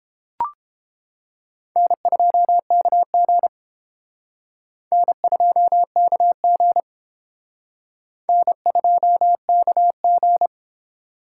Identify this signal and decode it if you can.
Morse code, heard as a keyed tone, about 29 wpm, 700 Hz. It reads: N2KG N2KG N2KG